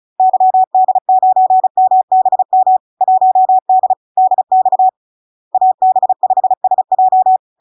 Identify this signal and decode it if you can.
Morse code, heard as a keyed tone, about 36 wpm, 750 Hz. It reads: YD9MBM 1D DX AB5SJ
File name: LARK-FD-2022-Log-in-CW-35wpm-750-Hz